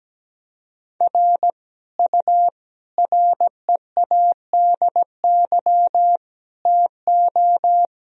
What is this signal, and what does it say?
Morse code, heard as a keyed tone, about 17 wpm, 700 Hz. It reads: R U READY TO
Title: Sentences from Top 400 Words and Abbreviations 17wpm